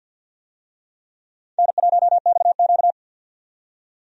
Morse code, heard as a keyed tone, about 48 wpm, 700 Hz. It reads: N1XQ
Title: USA Call Signs - 4 Characters - 50wpm